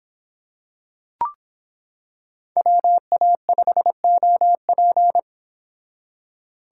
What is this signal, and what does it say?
Morse code, heard as a keyed tone, about 26 wpm, 700 Hz. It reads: WA5OP